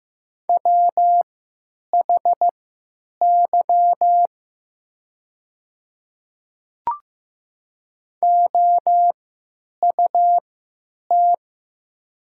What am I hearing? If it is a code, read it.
Morse code, heard as a keyed tone, about 15 wpm, spaced as for 9 wpm, 700 Hz. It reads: WHY OUT
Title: 3x Character Spacing - Top 300 Words - 15wpm